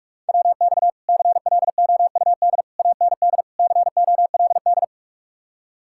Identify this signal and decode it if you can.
Morse code, heard as a keyed tone, about 49 wpm, 700 Hz. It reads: WX CLOUD AND COLD